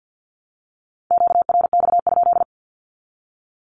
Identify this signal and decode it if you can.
Morse code, heard as a keyed tone, about 48 wpm, 700 Hz. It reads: QRX?
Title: Q-Codes 50wpm